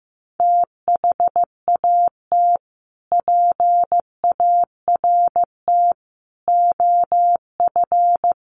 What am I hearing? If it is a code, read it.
Morse code, heard as a keyed tone, about 15 wpm, 700 Hz. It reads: THAT PART OF
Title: Sets of 5 Words Encore2 15wpm